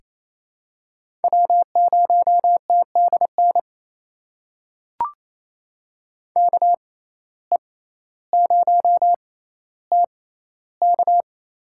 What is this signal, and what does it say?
Morse code, heard as a keyed tone, about 28 wpm, spaced as for 14 wpm, 700 Hz. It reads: W0TDN KE0TK
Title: USA Call Signs - 5 Characters - 6x Character Spacing - 28wpm